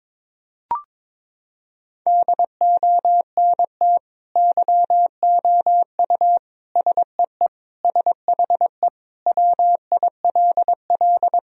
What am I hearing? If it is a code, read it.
Morse code, heard as a keyed tone, about 22 wpm, 700 Hz. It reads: DONT YOU SEE SHE WILL
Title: Sentences from Top 1000 Words - Review Part 2 - 22wpm